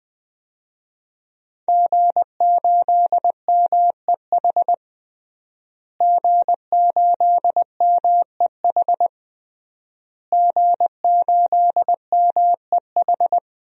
Morse code, heard as a keyed tone, about 21 wpm, 700 Hz. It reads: G8MEH G8MEH G8MEH
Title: DXCC-100 Call Signs - 5 Characters - Repeated 3x - 20wpm